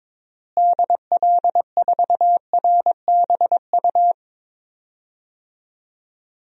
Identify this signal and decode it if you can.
Morse code, heard as a keyed tone, about 22 wpm, 700 Hz. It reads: DL4RBU
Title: DXCC-100 Call Signs - 6 Characters - Repeated 2x - 22wpm